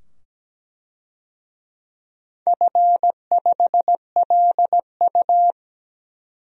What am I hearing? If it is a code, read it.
Morse code, heard as a keyed tone, about 17 wpm, 700 Hz. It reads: F5LU
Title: DXCC-100 Call Signs - 4 Characters - 17wpm